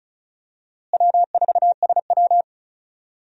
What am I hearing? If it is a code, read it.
Morse code, heard as a keyed tone, about 34 wpm, 700 Hz. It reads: W4SW
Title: USA Call Signs - 4 Characters - Repeated 3x - 35wpm